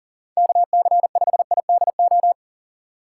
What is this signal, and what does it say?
Morse code, heard as a keyed tone, about 38 wpm, 700 Hz. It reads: KC5IDO
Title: USA Call Signs - 6 Characters - 2x Character Spacing - 40wpm